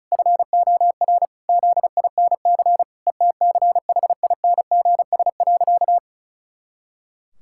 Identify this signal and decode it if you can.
Morse code, heard as a keyed tone, about 34 wpm, 700 Hz. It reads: FOR ZINC ETCHINGS.
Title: Pangram Challenge v2 - 35wpm